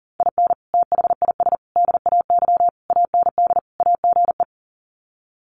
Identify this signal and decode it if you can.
Morse code, heard as a keyed tone, about 40 wpm, 700 Hz. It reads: IN THIS DAY AND AGE
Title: Sets of 5 Words Encore 40wpm